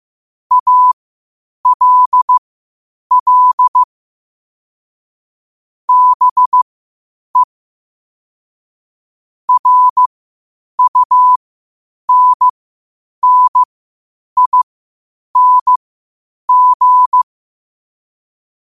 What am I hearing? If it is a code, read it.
Morse code, heard as a keyed tone, about 15 wpm, spaced as for 8 wpm, 1000 Hz. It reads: ALL BE RUNNING
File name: qotdfiles / 8wpm